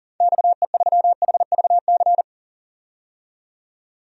Morse code, heard as a keyed tone, about 40 wpm, 700 Hz. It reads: XE3HVC
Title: DXCC-100 Call Signs - 6 Characters - 40wpm